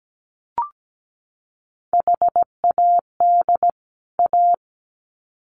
Morse code, heard as a keyed tone, about 17 wpm, 700 Hz. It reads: HAD A
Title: Sets of 2 Words 17wpm